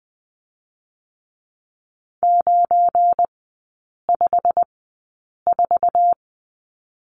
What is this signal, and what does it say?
Morse code, heard as a keyed tone, about 20 wpm, spaced as for 8 wpm, 700 Hz. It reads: 954
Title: 184 Lesson 30e - Letter b - Three Characters 20wpm